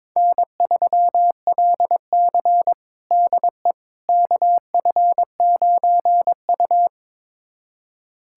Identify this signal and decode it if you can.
Morse code, heard as a keyed tone, about 21 wpm, 700 Hz. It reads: N3LC DE KF9U
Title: USA Callsign DE Callsign - 4 Characters - 22wpm